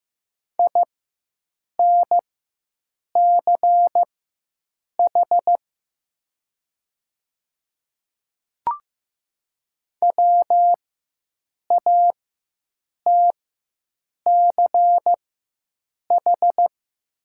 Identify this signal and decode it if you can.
Morse code, heard as a keyed tone, about 15 wpm, spaced as for 7 wpm, 700 Hz. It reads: INCH WATCH
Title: 4x Character Spacing - Top 500 Words - 15wpm